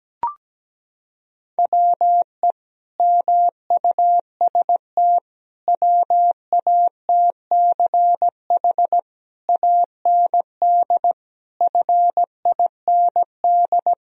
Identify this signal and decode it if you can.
Morse code, heard as a keyed tone, about 17 wpm, 700 Hz. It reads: WE MUST WATCH AND FIND
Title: Sentences from Top 1000 Words - Review Part 1 - 17wpm